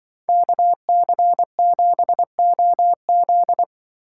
Morse code, heard as a keyed tone, about 24 wpm, 700 Hz. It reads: KC7OZ
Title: Speed Racing - USA 2x2 Call Signs 20wpm